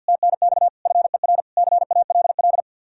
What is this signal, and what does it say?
Morse code, heard as a keyed tone, about 64 wpm, 700 Hz. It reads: TNX FER CALL